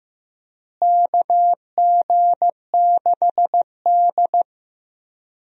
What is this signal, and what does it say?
Morse code, heard as a keyed tone, about 16 wpm, 700 Hz. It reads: KG6D
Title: USA Call Signs - 4 Characters - Repeated 2x - 15wpm